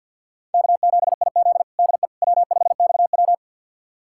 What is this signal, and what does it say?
Morse code, heard as a keyed tone, about 46 wpm, 700 Hz. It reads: K7IZ DE W5XW
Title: USA Callsign DE Callsign - 4 Characters - 50wpm